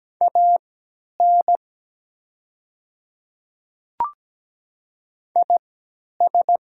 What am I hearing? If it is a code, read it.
Morse code, heard as a keyed tone, about 17 wpm, spaced as for 10 wpm, 700 Hz. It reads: AN IS
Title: 3x Character Spacing - Top 100 Words - 17wpm